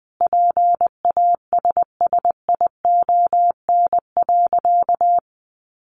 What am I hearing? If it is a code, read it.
Morse code, heard as a keyed tone, about 20 wpm, 700 Hz. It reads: PASSION.
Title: Sentences from Top 1300 Words 20wpm